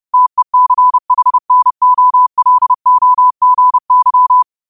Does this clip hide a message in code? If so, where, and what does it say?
Yes: Morse code from 0.1 to 4.4 s, heard as a keyed tone, about 30 wpm, 1000 Hz. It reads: TECHNOLOGY